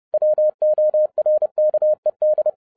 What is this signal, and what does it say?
Morse code, heard as a keyed tone, about 30 wpm, 610 Hz. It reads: WORKED
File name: 2012-01-mp3 / n12-I DIDNT MISS THE SOLAR CYCLE 24-30wpm